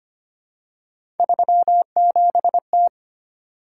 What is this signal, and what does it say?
Morse code, heard as a keyed tone, about 25 wpm, 700 Hz. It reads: 37T
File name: Three Letter-Number Combinations - 25wpm